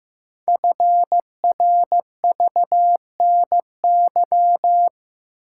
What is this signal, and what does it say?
Morse code, heard as a keyed tone, about 15 wpm, 700 Hz. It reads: FRVNY
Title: Code Groups - 5 Letters - 15wpm